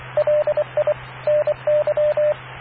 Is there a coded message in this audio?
Yes: Morse code from 0.2 to 2.3 s, heard as a keyed tone, about 23 wpm, 590 Hz, with radio noise. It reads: LI NY